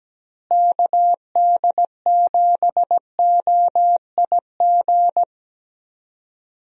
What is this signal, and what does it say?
Morse code, heard as a keyed tone, about 17 wpm, 700 Hz. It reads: KD7OIG
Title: USA Call Signs - 6 Characters - Repeated 3x - 17wpm